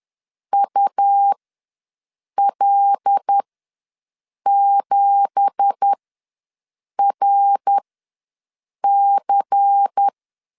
Morse code, heard as a keyed tone, about 10 wpm, spaced as for 6 wpm, 800 Hz. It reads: UL7RC
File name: LEZIONI / 9 2867 ws 10wpm 4215